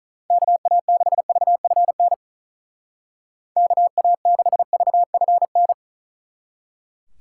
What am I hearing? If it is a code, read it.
Morse code, heard as a keyed tone, about 45 wpm, 700 Hz. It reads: KA6VFN KA6VFN